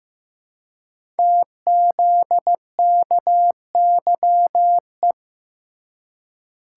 Morse code, heard as a keyed tone, about 15 wpm, 700 Hz. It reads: TZKYE
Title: Code Groups - 5 Letters - 15wpm